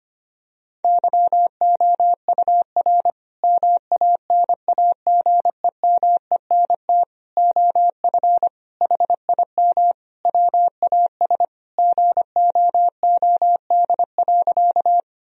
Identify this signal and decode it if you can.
Morse code, heard as a keyed tone, about 25 wpm, 700 Hz. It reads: YOUR MANAGEMENT OF HIM WAS GOOD.
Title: Sentences from Top 1000 Words - Review Part 4 - 25wpm